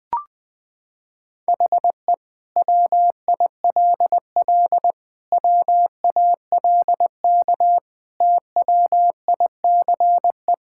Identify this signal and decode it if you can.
Morse code, heard as a keyed tone, about 20 wpm, 700 Hz. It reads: HE WILL WALK TWICE
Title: Sentences from Top 1600 Words 20wpm